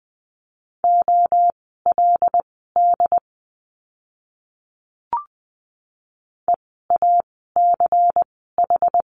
Morse code, heard as a keyed tone, about 20 wpm, spaced as for 14 wpm, 700 Hz. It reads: OLD EACH
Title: 230 Lesson 37f - Letter x - Words 20wpm